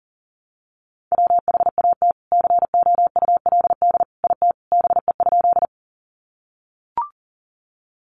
Morse code, heard as a keyed tone, about 40 wpm, 700 Hz. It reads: WHAT COULD IT BE?
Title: Sentences from Top 100 Words 40wpm